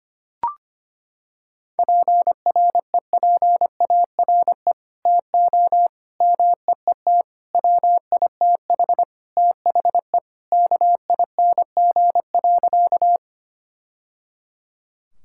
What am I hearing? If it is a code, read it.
Morse code, heard as a keyed tone, about 25 wpm, 700 Hz. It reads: PREPARE TO MEET WITH THE KING.
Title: Sentences from Top 1000 Words - Review Part 6 - 25wpm